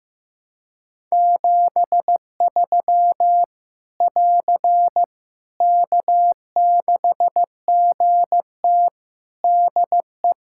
Morse code, heard as a keyed tone, about 15 wpm, 700 Hz. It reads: 73 <AR> K6GT DE
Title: Practice QSO - Exchange 4 - Ending - 15wpm